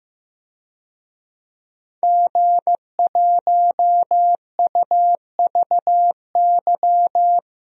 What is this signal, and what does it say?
Morse code, heard as a keyed tone, about 15 wpm, 700 Hz. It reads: G1UVY
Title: DXCC-100 Call Signs - 5 Characters - Repeated 2x - 15wpm